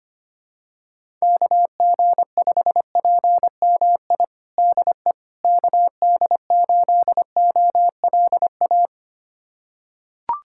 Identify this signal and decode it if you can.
Morse code, heard as a keyed tone, about 25 wpm, 700 Hz. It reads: KG5PMI DE KD8OLA
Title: USA Callsign DE Callsign - 6 Characters - 25wpm